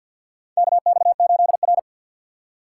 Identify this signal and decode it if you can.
Morse code, heard as a keyed tone, about 52 wpm, 700 Hz. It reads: KX8R
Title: USA Call Signs - 2x Character Spacing - 50wpm